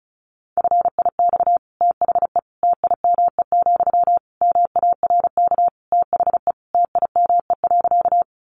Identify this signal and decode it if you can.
Morse code, heard as a keyed tone, about 34 wpm, 700 Hz. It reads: FIX THE TIME, MARK THE TIME.
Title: Sentences from Top 1400 Words 35wpm